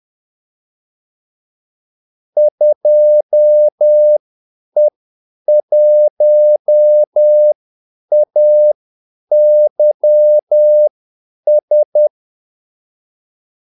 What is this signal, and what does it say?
Morse code, heard as a keyed tone, about 10 wpm, spaced as for 8 wpm, 600 Hz. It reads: 2E1AYS
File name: Callsigns-10-8wpm